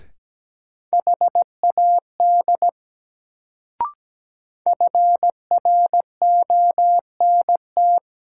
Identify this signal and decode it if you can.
Morse code, heard as a keyed tone, about 17 wpm, 700 Hz. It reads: HAD FRONT
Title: Sets of 2 Words - New Word Drill - 17wpm